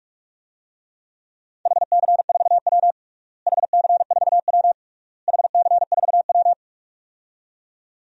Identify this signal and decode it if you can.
Morse code, heard as a keyed tone, about 46 wpm, 700 Hz. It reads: HC4W HC4W HC4W
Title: DXCC-100 Call Signs - 4 Characters - Repeated 3x - 45wpm